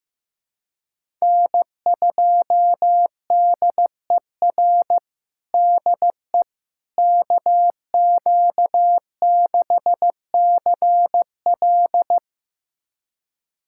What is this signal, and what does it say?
Morse code, heard as a keyed tone, about 15 wpm, 700 Hz. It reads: N2DER DE KQ6CL
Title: USA Callsign DE Callsign - 5 Characters - 15wpm